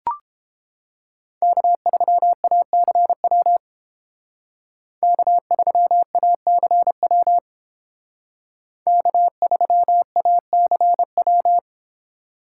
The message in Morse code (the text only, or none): K3ACW K3ACW K3ACW